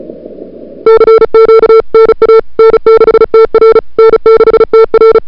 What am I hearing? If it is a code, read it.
Morse code, heard as a keyed tone, about 34 wpm, 440 Hz. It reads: CQ NA N6TR N6TR
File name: n6tr / Fab2007CWSprintHour1